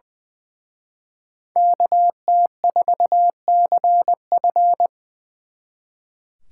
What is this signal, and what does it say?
Morse code, heard as a keyed tone, about 24 wpm, 700 Hz. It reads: KT4CF
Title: Speed Racing - USA Call Signs 20wpm